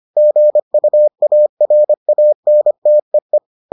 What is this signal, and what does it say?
Morse code, heard as a keyed tone, about 25 wpm, 600 Hz. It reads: GUARANTEE